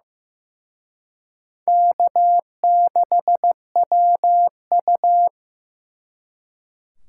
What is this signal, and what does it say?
Morse code, heard as a keyed tone, about 18 wpm, 700 Hz. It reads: K6WU
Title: Speed Racing - USA 1x2 Call Signs 15wpm